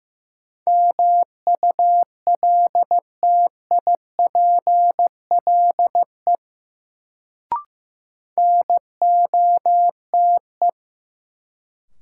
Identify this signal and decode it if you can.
Morse code, heard as a keyed tone, about 16 wpm, 700 Hz. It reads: MULTIPLE NOTE
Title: Sets of 5 Words - New Word Drill - 15wpm